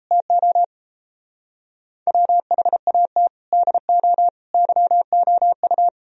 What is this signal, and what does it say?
Morse code, heard as a keyed tone, about 35 wpm, 700 Hz. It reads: TO WHAT DO YOU